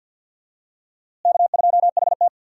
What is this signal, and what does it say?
Morse code, heard as a keyed tone, about 46 wpm, 700 Hz. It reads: K2HT
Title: USA Call Signs - 4 Characters - Repeated 3x - 50wpm